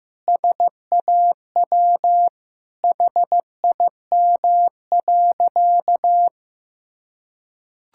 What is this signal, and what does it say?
Morse code, heard as a keyed tone, about 15 wpm, 700 Hz. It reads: SAW HIM.
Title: Sentences from Top 1600 Words 15wpm